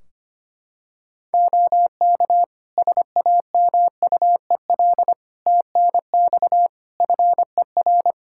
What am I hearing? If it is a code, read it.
Morse code, heard as a keyed tone, about 25 wpm, 700 Hz. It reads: OK SAMUEL TNX FER
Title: K1USN SST QSO Practice 25wpm